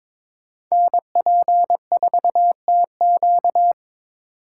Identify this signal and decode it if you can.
Morse code, heard as a keyed tone, about 26 wpm, 700 Hz. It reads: NP4TQ
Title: Speed Racing - USA 2x2 Call Signs 22wpm